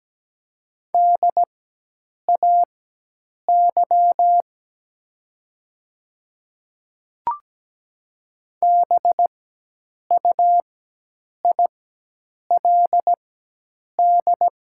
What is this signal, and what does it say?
Morse code, heard as a keyed tone, about 17 wpm, spaced as for 8 wpm, 700 Hz. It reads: DAY BUILD
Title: 4x Character Spacing - Top 200 Words - 17wpm